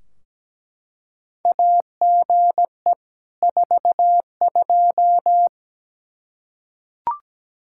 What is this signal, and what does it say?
Morse code, heard as a keyed tone, about 17 wpm, 700 Hz. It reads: AGE 42
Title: Practice QSO - Exchange 3 - More Info - 17wpm